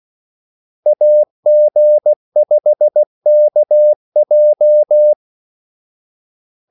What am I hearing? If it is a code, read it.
Morse code, heard as a keyed tone, about 16 wpm, 600 Hz. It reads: AG5KJ
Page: Kaldesignaler 16 wpm